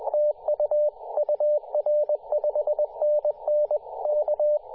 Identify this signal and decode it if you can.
Morse code, heard as a keyed tone, about 20 wpm, 600 Hz. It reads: TU UR 5NN K
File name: posts / 2017-sep-21-0819zt88xa-10-116mhz-cw